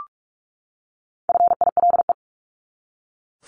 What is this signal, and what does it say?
Morse code, heard as a keyed tone, about 44 wpm, 700 Hz. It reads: FILE
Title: Top 1001-1250 Words 45wpm